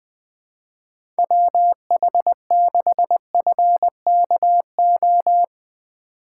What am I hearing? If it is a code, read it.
Morse code, heard as a keyed tone, about 20 wpm, 700 Hz. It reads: WH6FKO
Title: USA Call Signs - 6 Characters - 2x Character Spacing - 20wpm